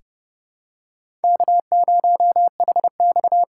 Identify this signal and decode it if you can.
Morse code, heard as a keyed tone, about 30 wpm, 700 Hz. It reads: K0HX
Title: USA Callsign DE Callsign 30wpm